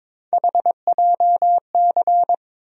Morse code, heard as a keyed tone, about 22 wpm, 700 Hz. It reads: HJC
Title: USA Call Sign Postfixes - 3 Letters 22wpm